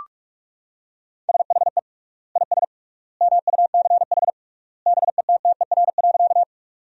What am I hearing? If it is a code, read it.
Morse code, heard as a keyed tone, about 43 wpm, 700 Hz. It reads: SHE IS MUCH BETTER.
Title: 3x Word-Spacing - Sentences from Top 400 Words - 45wpm